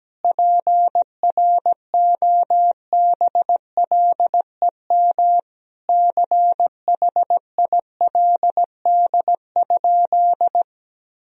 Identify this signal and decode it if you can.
Morse code, heard as a keyed tone, about 17 wpm, 700 Hz. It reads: PROBLEM CHILD?
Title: Sentences from Top 1000 Words - Review Part 2 - 17wpm